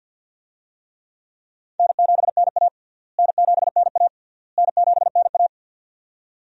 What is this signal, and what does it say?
Morse code, heard as a keyed tone, about 49 wpm, 700 Hz. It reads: N7NA N7NA N7NA
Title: USA Call Signs - 4 Characters - Repeated 3x - 50wpm